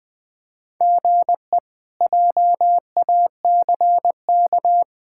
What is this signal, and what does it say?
Morse code, heard as a keyed tone, about 21 wpm, 700 Hz. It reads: GE JACK
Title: K1USN SST QSO Practice - 20wpm